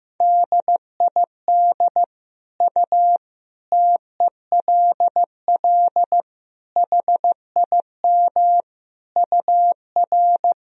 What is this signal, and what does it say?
Morse code, heard as a keyed tone, about 15 wpm, 700 Hz. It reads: DID U TELL HIM UR
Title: Sentences from Top 400 Words and Abbreviations 15wpm